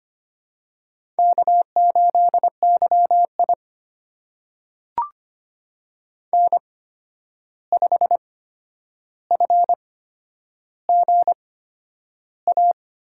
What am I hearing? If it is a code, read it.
Morse code, heard as a keyed tone, about 25 wpm, spaced as for 10 wpm, 700 Hz. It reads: K8YI N5FGA
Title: USA Call Signs - 8x Character Spacing - 25wpm